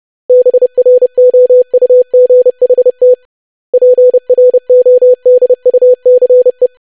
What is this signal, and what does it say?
Morse code, heard as a keyed tone, about 30 wpm, 500 Hz. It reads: BROUGHT PRODUCE
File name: lesson / web / basic500-25wpm